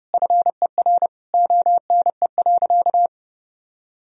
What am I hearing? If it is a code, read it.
Morse code, heard as a keyed tone, about 32 wpm, 700 Hz. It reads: FER ONE.
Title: Sentences from Top 300 Words and Abbreviations 30wpm